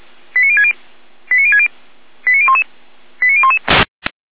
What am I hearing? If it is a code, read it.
Five-tone selective call (ZVEI), heard as quick runs of five tones, tones of about 70 ms. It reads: 89060 89060 89011 89011